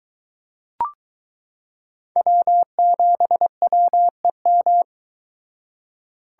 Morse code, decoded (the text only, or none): W7WEM